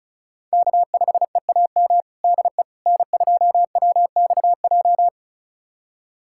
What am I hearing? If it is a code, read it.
Morse code, heard as a keyed tone, about 36 wpm, 700 Hz. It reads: K5EAM DE N2WXJ
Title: USA Callsign DE Callsign - 5 Characters - 35wpm